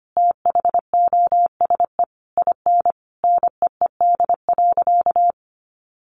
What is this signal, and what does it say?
Morse code, heard as a keyed tone, about 25 wpm, 700 Hz. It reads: THOSE IN NEED.
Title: Sentences from Top 1200 Words 25wpm